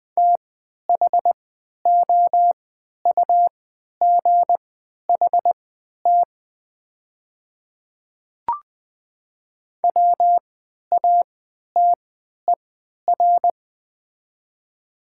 Morse code, heard as a keyed tone, about 20 wpm, spaced as for 11 wpm, 700 Hz. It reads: THOUGHT WATER
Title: 3x Character Spacing - Top 300 Words - 20wpm